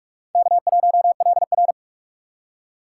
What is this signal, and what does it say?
Morse code, heard as a keyed tone, about 41 wpm, 700 Hz. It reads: K1LR